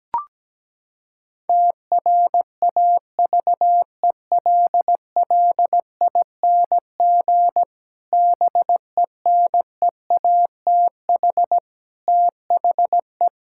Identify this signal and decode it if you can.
Morse code, heard as a keyed tone, about 17 wpm, 700 Hz. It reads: TRAVELLING BENEATH THE
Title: Pangram Challenge v2 - 17wpm